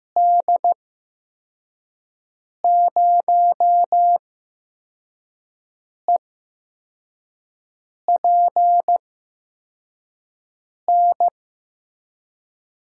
Morse code, heard as a keyed tone, about 15 wpm, spaced as for 4 wpm, 700 Hz. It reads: D0EPN
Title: USA Call Signs - 6 Characters - 8x Character Spacing - 15wpm